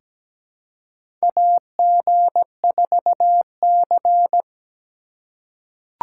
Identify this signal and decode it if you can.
Morse code, heard as a keyed tone, about 17 wpm, 700 Hz. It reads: AG4C
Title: USA Call Signs - 4 Characters - 3x Character Spacing - 17wpm